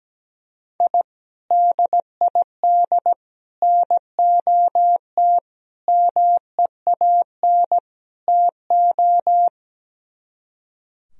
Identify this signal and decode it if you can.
Morse code, heard as a keyed tone, about 17 wpm, 700 Hz. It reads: I DID NOT MEAN TO
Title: Speed-Racing - Sets of 5 Words 17wpm